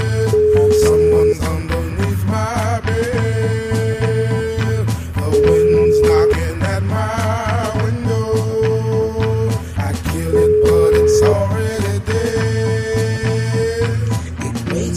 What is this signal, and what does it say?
Ringback tone (Europe, 425 Hz), with noise.